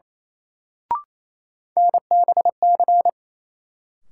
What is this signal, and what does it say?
Morse code, heard as a keyed tone, about 28 wpm, 700 Hz. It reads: NBC
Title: Sets of 5 Words Encore - New Word Drill - 28wpm